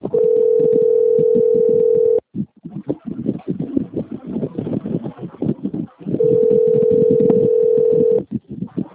Ringback tone (North America), 440 + 480 Hz.
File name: out- - - - - .27216